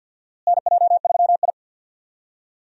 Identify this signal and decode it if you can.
Morse code, heard as a keyed tone, about 52 wpm, 700 Hz. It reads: NJ3I
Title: USA Call Signs - 4 Characters - Repeated 2x - 50wpm